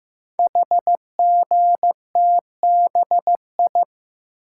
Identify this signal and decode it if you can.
Morse code, heard as a keyed tone, about 15 wpm, 700 Hz. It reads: HGTBI
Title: Code Groups - 5 Letters - 15wpm